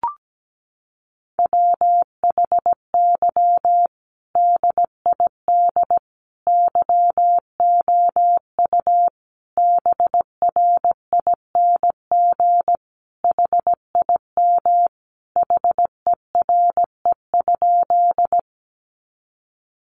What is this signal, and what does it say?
Morse code, heard as a keyed tone, about 17 wpm, 700 Hz. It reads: WHY DID YOU BRING HIM HERE?
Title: Sentences from Top 500 Words 17wpm